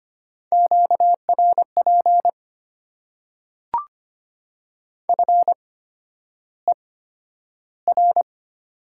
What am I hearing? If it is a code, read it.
Morse code, heard as a keyed tone, about 25 wpm, spaced as for 11 wpm, 700 Hz. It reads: QRP FER
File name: Top 90 QSO Elements - 8x Character Spacing - 25wpm